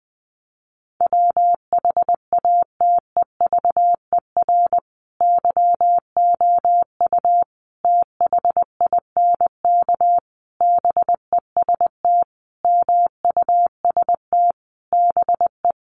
Morse code, heard as a keyed tone, about 20 wpm, 700 Hz. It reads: WHATEVER YOU THINK BEST MUST BE
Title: Sentences from Top 1100 Words 20wpm